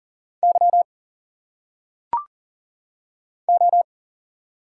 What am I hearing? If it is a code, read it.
Morse code, heard as a keyed tone, about 40 wpm, 700 Hz. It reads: Y O
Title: Single Letters - Phonetic Pronunciation 40wpm